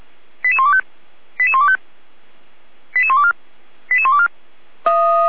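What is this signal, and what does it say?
Five-tone selective call (ZVEI), heard as quick runs of five tones, tones of about 70 ms. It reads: 88125 88125 88124 88124
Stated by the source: ZVEI five-tone selective call received via SDR.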